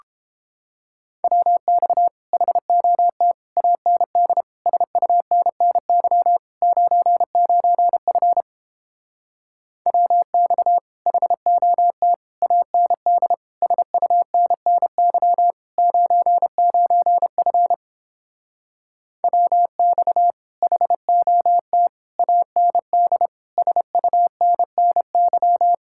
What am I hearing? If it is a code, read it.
Morse code, heard as a keyed tone, about 29 wpm, 700 Hz. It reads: WX HOT AND SUNNY 99F WX HOT AND SUNNY 99F WX HOT AND SUNNY